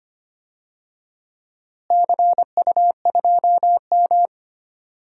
Morse code, heard as a keyed tone, about 25 wpm, 700 Hz. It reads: CU2M
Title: DXCC-100 Call Signs - 4 Characters - Repeated 2x - 25wpm